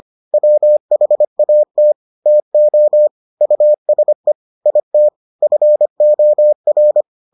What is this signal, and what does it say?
Morse code, heard as a keyed tone, about 25 wpm, 600 Hz. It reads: WHAT TO USE IT FOR